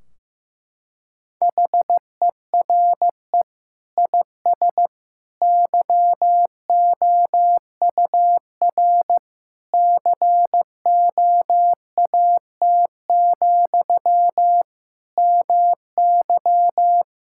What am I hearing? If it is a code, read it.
Morse code, heard as a keyed tone, about 15 wpm, 700 Hz. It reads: HERE IS YOUR COAT, MY
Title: Sentences from Top 1600 Words 15wpm